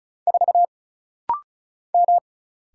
Morse code, heard as a keyed tone, about 34 wpm, 700 Hz. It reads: 4 M
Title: Single Letter-Number - Rapid-Fire 35wpm